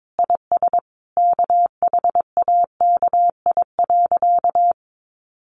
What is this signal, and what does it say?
Morse code, heard as a keyed tone, about 22 wpm, 700 Hz. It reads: IS KHAKI.
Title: Additional Easy Phrases 22wpm